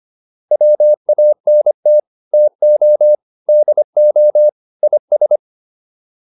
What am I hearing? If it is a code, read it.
Morse code, heard as a keyed tone, about 25 wpm, 600 Hz. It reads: WANT TO DO IS